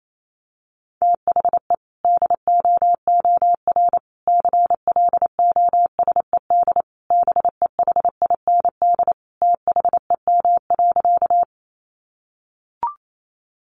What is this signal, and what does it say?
Morse code, heard as a keyed tone, about 28 wpm, 700 Hz. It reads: THE DOOR CLOSED BEHIND THEM.
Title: Sentences from Top 1400 Words 28wpm